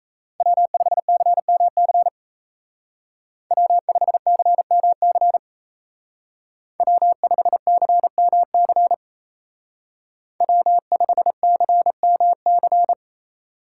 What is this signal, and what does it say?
Morse code, heard as a keyed tone, about 36 wpm, 700 Hz. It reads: W5CMC W5CMC W5CMC W5CMC